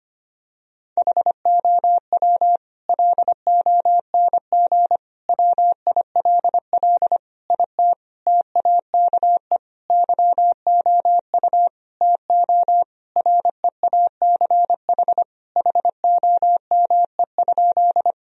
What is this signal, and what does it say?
Morse code, heard as a keyed tone, about 25 wpm, 700 Hz. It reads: HOW LONG WILL IT TAKE YOU TO REACH HOME?
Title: Sentences from Top 1000 Words - Review Part 2 - 25wpm